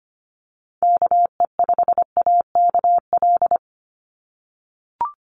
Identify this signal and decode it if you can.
Morse code, heard as a keyed tone, about 25 wpm, 700 Hz. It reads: KE5AKL
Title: NA SOTA Chaser Callsigns 25wpm